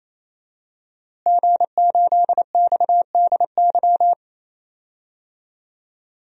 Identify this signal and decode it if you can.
Morse code, heard as a keyed tone, about 28 wpm, 700 Hz. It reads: G8XDY
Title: DXCC-100 Call Signs - 5 Characters - Repeated 3x - 28wpm